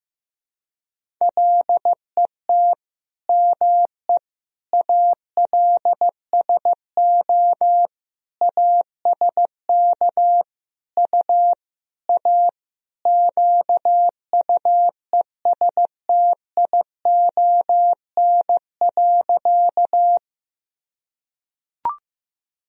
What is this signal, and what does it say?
Morse code, heard as a keyed tone, about 16 wpm, 700 Hz. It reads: LET ME ALSO ASK U A QUESTION.
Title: Sentences from Top 400 Words and Abbreviations 15wpm